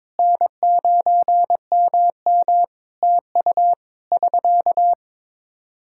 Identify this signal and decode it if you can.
Morse code, heard as a keyed tone, about 22 wpm, 700 Hz. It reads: N9MM TU <SK>
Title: K1USN SST QSO Practice 22wpm